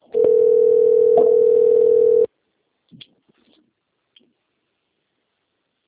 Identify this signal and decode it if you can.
Ringback tone (North America), 440 + 480 Hz.